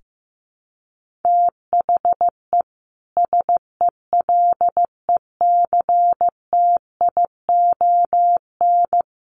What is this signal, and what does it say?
Morse code, heard as a keyed tone, about 15 wpm, 700 Hz. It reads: THE SELECTION
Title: Sentences from Top 1400 Words 15wpm